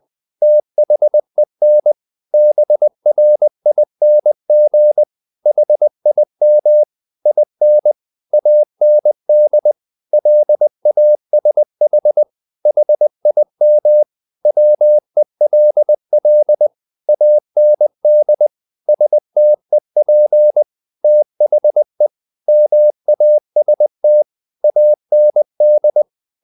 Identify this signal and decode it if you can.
Morse code, heard as a keyed tone, about 20 wpm, 600 Hz. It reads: THEN BRING HIM IN AND LASH HIM WELL AND STEP THE MAST AND
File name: the old man and the sea part 2 20wpm